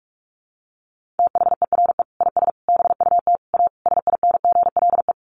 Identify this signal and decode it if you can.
Morse code, heard as a keyed tone, about 44 wpm, 700 Hz. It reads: THERE IS BUT A SINGLE
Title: Sentences from Top 1400 Words 45wpm